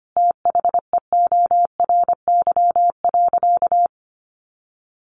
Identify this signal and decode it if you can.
Morse code, heard as a keyed tone, about 25 wpm, 700 Hz. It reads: THEORY.
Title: Sentences from Top 1400 Words 25wpm